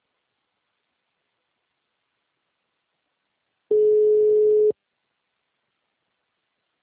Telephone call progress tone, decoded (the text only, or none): ringback tone